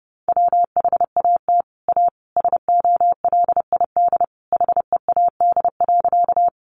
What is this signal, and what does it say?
Morse code, heard as a keyed tone, about 30 wpm, 700 Hz. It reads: WHAT A SOLID HEAD.
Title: Sentences from Top 1500 Words 30wpm